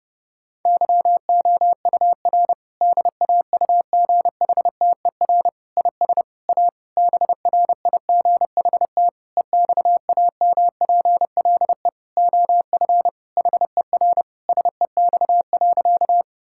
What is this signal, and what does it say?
Morse code, heard as a keyed tone, about 30 wpm, 700 Hz. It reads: YOUR DAUGHTER IS A BRIGHT EXAMPLE OF HER SEX.
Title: Sentences from Top 1600 Words 30wpm